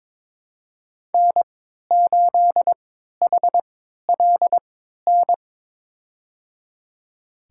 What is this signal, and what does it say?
Morse code, heard as a keyed tone, about 21 wpm, spaced as for 12 wpm, 700 Hz. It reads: N8HLN
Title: USA Call Signs - 3x Character Spacing - 22wpm